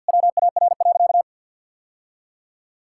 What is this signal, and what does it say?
Morse code, heard as a keyed tone, about 52 wpm, 700 Hz. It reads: WAR.
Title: Sentences from Top 1000 Words - Review Part 6 - 50wpm